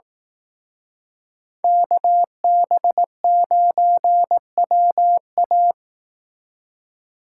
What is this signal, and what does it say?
Morse code, heard as a keyed tone, about 18 wpm, 700 Hz. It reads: KB9WA
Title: Speed Racing - USA 2x2 Call Signs 15wpm